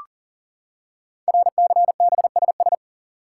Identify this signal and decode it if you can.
Morse code, heard as a keyed tone, about 39 wpm, 700 Hz. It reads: RCBSS
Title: Code Groups - 5 Letters - 40wpm